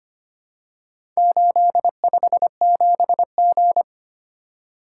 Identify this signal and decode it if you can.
Morse code, heard as a keyed tone, about 25 wpm, 700 Hz. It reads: 857G
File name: practice / 25wpm / Four Letter-Number Combinations - 25wpm